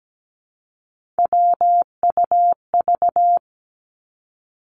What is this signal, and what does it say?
Morse code, heard as a keyed tone, about 17 wpm, 700 Hz. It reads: WUV
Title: Three Challenging Letter-Number Combinations 17wpm